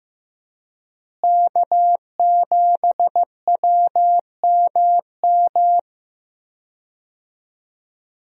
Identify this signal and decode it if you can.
Morse code, heard as a keyed tone, about 15 wpm, 700 Hz. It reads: K7WMM
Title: USA Call Signs - 5 Characters - Repeated 3x - 15wpm